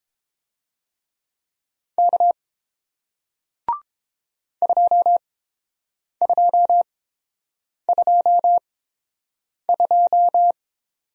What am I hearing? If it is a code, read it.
Morse code, heard as a keyed tone, about 31 wpm, spaced as for 7 wpm, 700 Hz. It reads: K 2222